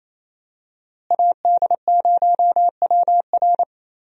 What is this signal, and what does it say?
Morse code, heard as a keyed tone, about 28 wpm, 700 Hz. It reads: AD0WR
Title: USA Call Signs - 5 Characters - Repeated 3x - 28wpm